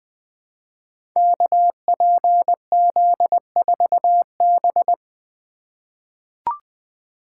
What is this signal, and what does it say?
Morse code, heard as a keyed tone, about 20 wpm, 700 Hz. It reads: KPZ4B
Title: Code Groups - 5 Letter:Numbers - 20wpm